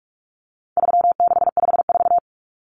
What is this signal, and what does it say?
Morse code, heard as a keyed tone, about 44 wpm, 700 Hz. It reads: 3654
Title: Common Cut Numbers - 4-Digit 45wpm